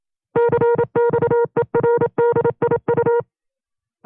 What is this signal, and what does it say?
Morse code, heard as a keyed tone, about 27 wpm, 470 Hz. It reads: CXERDIU
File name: FreeFiles / CW-practice / 25WPM-side1